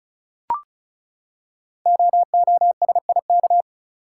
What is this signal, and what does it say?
Morse code, heard as a keyed tone, about 36 wpm, 690 Hz. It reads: OOSIK
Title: Code Groups - 5 Letters - 35wpm